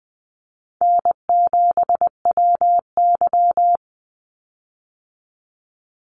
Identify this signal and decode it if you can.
Morse code, heard as a keyed tone, about 20 wpm, 700 Hz. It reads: N7WY
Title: Easy Call Signs - 20wpm